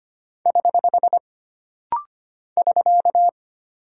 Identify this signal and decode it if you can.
Morse code, heard as a keyed tone, about 25 wpm, 700 Hz. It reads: <HH> <SK>
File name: Single Punctuation-Prosign - Rapid-Fire - 25wpm